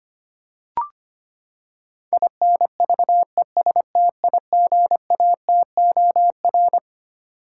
Morse code, heard as a keyed tone, about 25 wpm, 700 Hz. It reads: INVESTIGATOR
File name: Twelve Letter Words - 25wpm